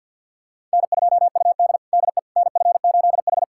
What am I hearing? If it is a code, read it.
Morse code, heard as a keyed tone, about 49 wpm, 700 Hz. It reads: N2UD DE NF8H
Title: USA Callsign DE Callsign - 4 Characters - 50wpm